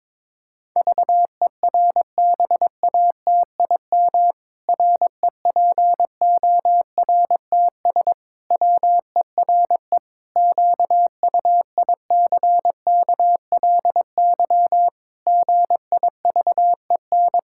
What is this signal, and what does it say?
Morse code, heard as a keyed tone, about 22 wpm, 700 Hz. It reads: VERBATIM REPORTS WERE QUICKLY GIVEN
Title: Pangram Challenge v2 - 22wpm